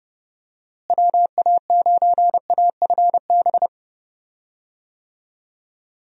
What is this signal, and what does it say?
Morse code, heard as a keyed tone, about 32 wpm, 700 Hz. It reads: WA9AFB
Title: USA Call Signs - 6 Characters - Repeated 3x - 30wpm